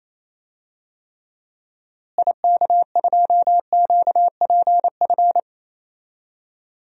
Morse code, heard as a keyed tone, about 28 wpm, 700 Hz. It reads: IK2QPF
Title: DXCC-100 Call Signs - 6 Characters - Repeated 2x - 28wpm